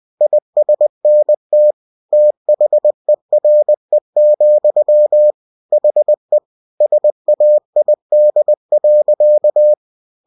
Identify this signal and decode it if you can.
Morse code, heard as a keyed tone, about 20 wpm, 600 Hz. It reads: ISNT THERE, HE SAID.